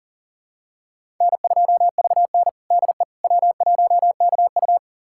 Morse code, heard as a keyed tone, about 40 wpm, 700 Hz. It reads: N2VN DE W1KU
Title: USA Callsign DE Callsign 40wpm